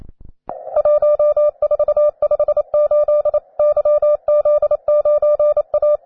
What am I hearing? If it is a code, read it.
Morse code, heard as a keyed tone, about 28 wpm, 600 Hz. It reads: 1458 YZ9A
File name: wae / cw beginner / jh3aiu cw